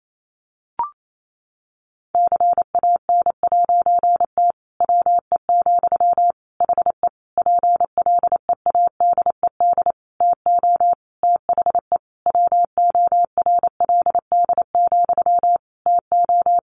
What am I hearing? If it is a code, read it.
Morse code, heard as a keyed tone, about 28 wpm, 700 Hz. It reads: CAN'T WE, HE PLEADED TO THE WORLD, TO
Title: A Sound Of Thunder 28wpm